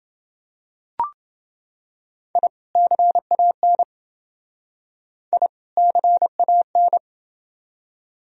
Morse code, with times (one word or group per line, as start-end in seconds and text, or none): 2.4-2.5 I
2.8-3.8 CAN
5.3-5.5 I
5.8-7.0 CAN